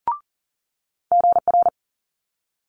Morse code, heard as a keyed tone, about 38 wpm, 700 Hz. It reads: GR
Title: Multi-letter Phonemes Quick Fire 40wpm